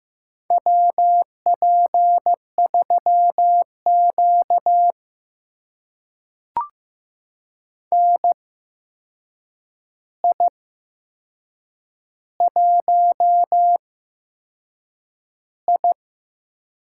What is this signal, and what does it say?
Morse code, heard as a keyed tone, about 15 wpm, spaced as for 6 wpm, 700 Hz. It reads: WP3Q NI1I
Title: USA Call Signs - 4 Characters - 8x Character Spacing - 15wpm